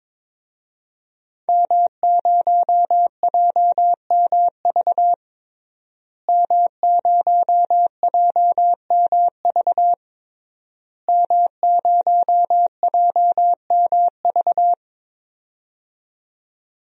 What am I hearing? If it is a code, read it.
Morse code, heard as a keyed tone, about 22 wpm, 700 Hz. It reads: M0JMV M0JMV M0JMV
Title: DXCC-100 Call Signs - 5 Characters - Repeated 3x - 22wpm